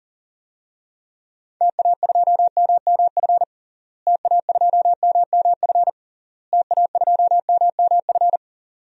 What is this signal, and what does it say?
Morse code, heard as a keyed tone, about 40 wpm, 700 Hz. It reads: TA2MMF TA2MMF TA2MMF
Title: DXCC-100 Call Signs - 6 Characters - Repeated 3x - 40wpm